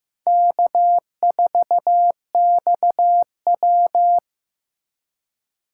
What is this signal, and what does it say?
Morse code, heard as a keyed tone, about 18 wpm, 700 Hz. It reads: K4XW
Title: Speed Racing - USA 1x2 Call Signs 15wpm